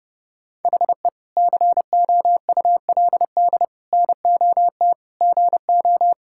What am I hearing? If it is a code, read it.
Morse code, heard as a keyed tone, about 30 wpm, 700 Hz. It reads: HE COULD NOT GO
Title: Sentences from Top 1000 Words - Review Part 6 - 30wpm